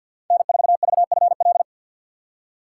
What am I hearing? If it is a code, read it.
Morse code, heard as a keyed tone, about 52 wpm, 700 Hz. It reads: N4VFL
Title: USA Call Signs - 5 Characters - 4x Character Spacing - 50wpm